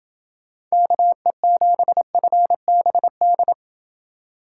Morse code, heard as a keyed tone, about 27 wpm, 700 Hz. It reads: KE7FBD